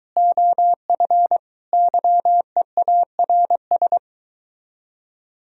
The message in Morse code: OF YEARS

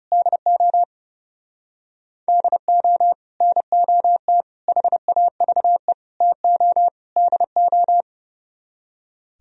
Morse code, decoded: DO DO NOT HAVE TO DO

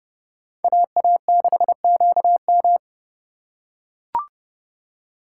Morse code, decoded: AA6QM

